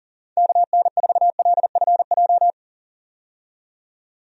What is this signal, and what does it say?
Morse code, heard as a keyed tone, about 40 wpm, 690 Hz. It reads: KN4LFJ
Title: USA Call Signs - 6 Characters - 40wpm